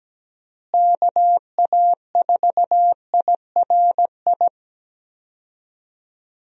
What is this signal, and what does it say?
Morse code, heard as a keyed tone, about 17 wpm, 700 Hz. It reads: KA4IRI
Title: USA Call Signs - 6 Characters - Repeated 2x - 17wpm